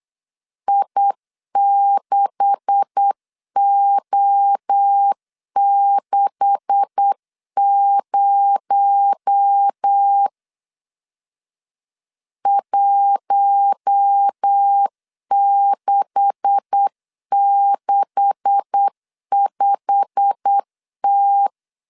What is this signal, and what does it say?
Morse code, heard as a keyed tone, about 8 wpm, 800 Hz. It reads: I6O60 1665T
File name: LEZIONI / 9 286 8wpm 3425